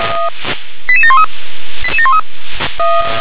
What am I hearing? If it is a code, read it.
Five-tone selective call (ZVEI), heard as quick runs of five tones, tones of about 70 ms. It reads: 88713